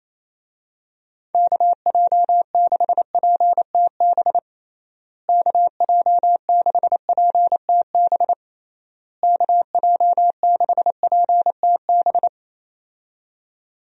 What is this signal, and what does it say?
Morse code, heard as a keyed tone, about 28 wpm, 700 Hz. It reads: KJ6PTB KJ6PTB KJ6PTB
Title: USA Call Signs - 6 Characters - Repeated 3x - 28wpm